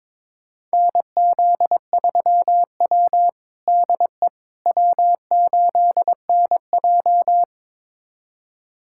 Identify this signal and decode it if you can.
Morse code, heard as a keyed tone, about 22 wpm, 700 Hz. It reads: NZ3W DE W8NJ
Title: USA Callsign DE Callsign - 4 Characters - 22wpm